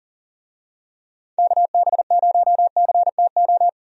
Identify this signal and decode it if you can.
Morse code, heard as a keyed tone, about 40 wpm, 700 Hz. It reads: KB0CTO
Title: USA Call Signs - 6 Characters - 40wpm